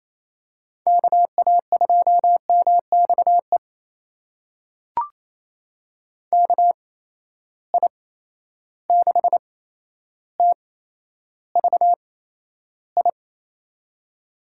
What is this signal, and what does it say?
Morse code, heard as a keyed tone, about 28 wpm, spaced as for 12 wpm, 700 Hz. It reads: KA2MXE KI6TVI
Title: USA Call Signs - 6 Characters - 8x Character Spacing - 28wpm